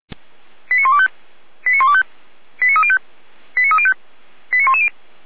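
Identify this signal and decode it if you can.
Five-tone selective call (ZVEI), heard as quick runs of five tones, tones of about 70 ms. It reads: 89125 89125 89305 89305 89110
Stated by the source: ZVEI five-tone selective call, code 89277, received by SDR; Feuerwehr Orlamünde fire brigade.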